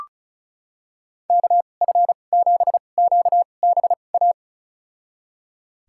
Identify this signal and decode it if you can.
Morse code, heard as a keyed tone, about 36 wpm, spaced as for 25 wpm, 700 Hz. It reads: KF7QBA
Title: USA Call Signs - 2x Character Spacing - 35wpm